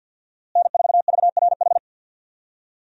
Morse code, heard as a keyed tone, about 49 wpm, 700 Hz. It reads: N4VRH
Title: USA Call Signs - 5 Characters - Repeated 2x - 50wpm